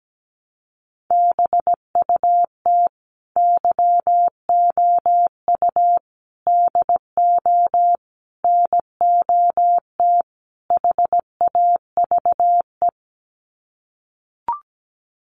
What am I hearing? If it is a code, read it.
Morse code, heard as a keyed tone, about 17 wpm, 690 Hz. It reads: BUT YOU DO NOT HAVE
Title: Sets of 5 Words Encore 17wpm